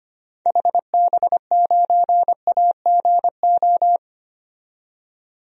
Morse code, heard as a keyed tone, about 25 wpm, 700 Hz. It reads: HB9AGO
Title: DXCC-100 Call Signs - 6 Characters - 25wpm